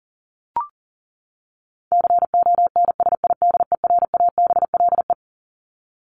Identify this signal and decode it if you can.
Morse code, heard as a keyed tone, about 40 wpm, 700 Hz. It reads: CONSIDERABLE
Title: LY Suffix - Top 100 Words with Base Words 40wpm